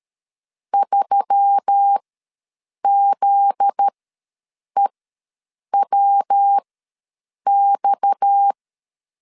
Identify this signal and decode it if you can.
Morse code, heard as a keyed tone, about 13 wpm, spaced as for 7 wpm, 800 Hz. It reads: 3ZEWX
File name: LEZIONI / 9 2867 ws 12wpm 3431